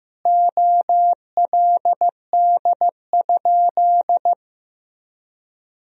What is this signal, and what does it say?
Morse code, heard as a keyed tone, about 15 wpm, 700 Hz. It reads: OLD?
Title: 3x Word-Spacing - Sentences from Top 400 Words - 15wpm